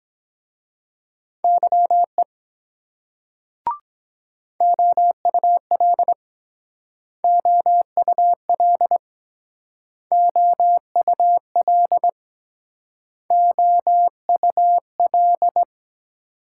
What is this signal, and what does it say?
Morse code, heard as a keyed tone, about 23 wpm, 700 Hz. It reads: YE OUL OUL OUL OUL